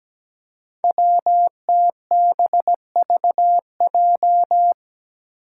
Morse code, heard as a keyed tone, about 17 wpm, 700 Hz. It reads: WTBVJ
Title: Five Challenging Letter-Number Combinations 17wpm